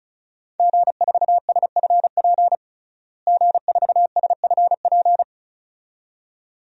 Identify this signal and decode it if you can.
Morse code, heard as a keyed tone, about 36 wpm, 700 Hz. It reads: G4SFP G4SFP
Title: DXCC-100 Call Signs - 5 Characters - Repeated 2x - 35wpm